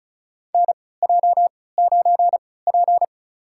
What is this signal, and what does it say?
Morse code, heard as a keyed tone, about 36 wpm, spaced as for 20 wpm, 700 Hz. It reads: NJ9P
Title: USA Call Signs - 4 Characters - 3x Character Spacing - 35wpm